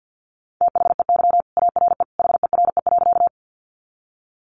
Morse code, heard as a keyed tone, about 50 wpm, 700 Hz. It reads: THEY ARE HERE.
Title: Sentences from Top 200 Words 50wpm